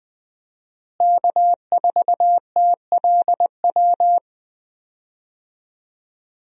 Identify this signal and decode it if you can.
Morse code, heard as a keyed tone, about 20 wpm, 700 Hz. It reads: K4TLW
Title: USA Call Signs - 5 Characters - Repeated 2x - 20wpm